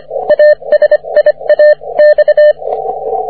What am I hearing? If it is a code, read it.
Morse code, heard as a keyed tone, about 25 wpm, 600 Hz, with radio noise. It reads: ASIA X
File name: wp / ts7c 30m cw